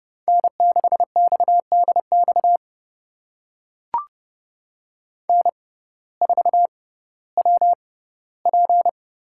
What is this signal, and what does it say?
Morse code, heard as a keyed tone, about 32 wpm, spaced as for 17 wpm, 700 Hz. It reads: N6XDX N4WP
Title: USA Call Signs - 5 Characters - 6x Character Spacing - 30wpm